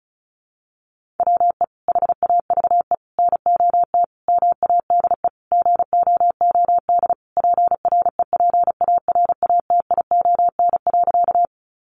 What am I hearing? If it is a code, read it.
Morse code, heard as a keyed tone, about 34 wpm, 700 Hz. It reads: WE HAVE NOT MADE GOOD PREPARATION.
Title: Sentences from Top 1400 Words 35wpm